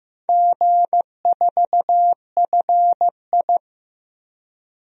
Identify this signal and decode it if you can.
Morse code, heard as a keyed tone, about 15 wpm, 700 Hz. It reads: G4FI
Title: DXCC-100 Call Signs - 4 Characters - 15wpm